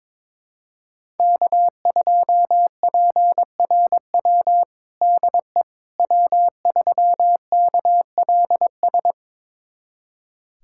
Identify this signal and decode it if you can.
Morse code, heard as a keyed tone, about 22 wpm, 700 Hz. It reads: K2PRW DE W3KLS
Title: USA Callsign DE Callsign - 5 Characters - 22wpm